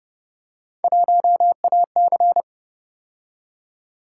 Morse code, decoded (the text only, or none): 1AC